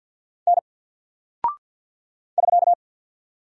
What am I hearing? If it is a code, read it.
Morse code, heard as a keyed tone, about 50 wpm, 700 Hz. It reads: N <SK>
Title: Single Character (Letter:Number:Punctuation:Prosign) 50wpm